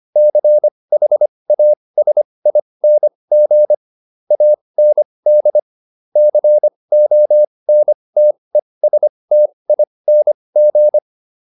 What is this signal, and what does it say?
Morse code, heard as a keyed tone, about 25 wpm, spaced as for 20 wpm, 600 Hz. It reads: CHASING AND CONTESTING
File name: SS206 20